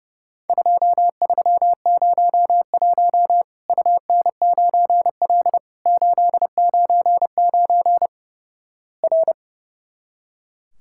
Morse code, heard as a keyed tone, about 32 wpm, 700 Hz. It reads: 2301 UN9L 899 R
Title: Worked All Europe DX Contest - QTC Exchange - 30wpm